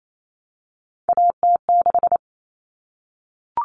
Morse code, heard as a keyed tone, about 28 wpm, 700 Hz. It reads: AT6
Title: All Cut Numbers - 3-Digit 28wpm